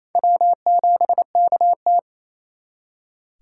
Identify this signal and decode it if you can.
Morse code, heard as a keyed tone, about 34 wpm, 700 Hz. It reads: W7KT
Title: Speed Racing - USA 1x2 Call Signs 28wpm